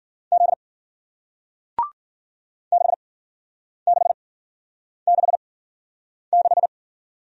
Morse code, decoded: Z 6666

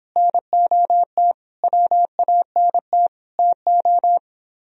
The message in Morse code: NOT WANT TO

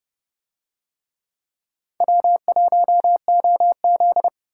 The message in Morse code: W1OZ

